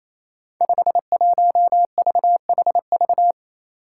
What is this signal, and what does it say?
Morse code, heard as a keyed tone, about 28 wpm, 700 Hz. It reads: 51VHV
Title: Five Challenging Letter-Number Combinations 28wpm